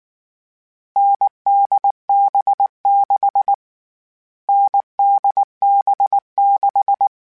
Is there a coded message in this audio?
Yes: Morse code from 1.0 to 7.1 s, heard as a keyed tone, about 19 wpm, 800 Hz. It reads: NDB6 NDB6